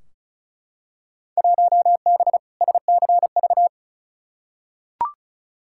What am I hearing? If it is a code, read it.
Morse code, heard as a keyed tone, about 36 wpm, 700 Hz. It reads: 1B SCV
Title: Field Day Exchanges Encore - 35wpm